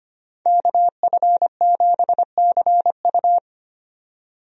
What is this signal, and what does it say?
Morse code, heard as a keyed tone, about 25 wpm, 700 Hz. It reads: KF7CU
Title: USA Call Signs - 5 Characters - 25wpm